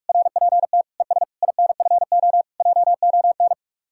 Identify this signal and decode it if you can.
Morse code, heard as a keyed tone, about 53 wpm, 700 Hz. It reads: RPT ES INFO JON
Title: Speed-Racing Common QSO Phrases - 45wpm